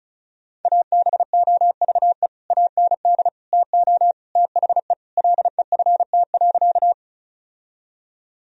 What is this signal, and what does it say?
Morse code, heard as a keyed tone, about 36 wpm, 700 Hz. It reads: ABOVE AND TO THE LEFT.
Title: Sentences from Top 1000 Words - Review Part 2 - 35wpm